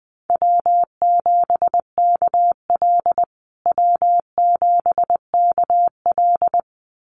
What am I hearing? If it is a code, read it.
Morse code, heard as a keyed tone, about 20 wpm, 700 Hz. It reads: W7KL W7KL
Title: USA Call Signs Repeated 2x 20wpm